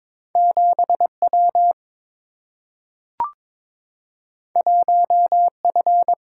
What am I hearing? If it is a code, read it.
Morse code, heard as a keyed tone, about 22 wpm, 700 Hz. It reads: 7 W 1 F
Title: Single-Letter Single-Number 22wpm